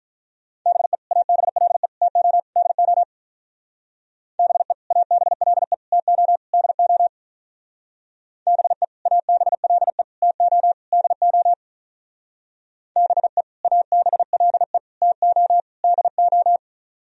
Morse code, heard as a keyed tone, about 46 wpm, 700 Hz. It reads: BE ABLE TO DO BE ABLE TO DO BE ABLE TO DO BE ABLE TO DO